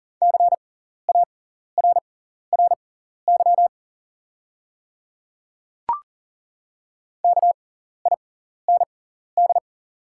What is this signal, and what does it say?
Morse code, heard as a keyed tone, about 40 wpm, spaced as for 14 wpm, 700 Hz. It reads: CARRY KIND
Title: 6x Character Spacing - Top 500 Words - 40wpm